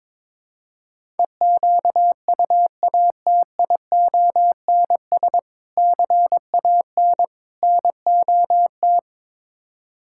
Morse code, heard as a keyed tone, about 22 wpm, 700 Hz. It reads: EQUATIONS CAN NOT
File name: practice / 22wpm / Build A Phrase - 5-Word Phrases - 22wpm